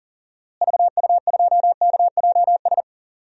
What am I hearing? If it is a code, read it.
Morse code, heard as a keyed tone, about 38 wpm, 700 Hz. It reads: VU2KJS
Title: DXCC-100 Call Signs - 6 Characters - Repeated 3x - 40wpm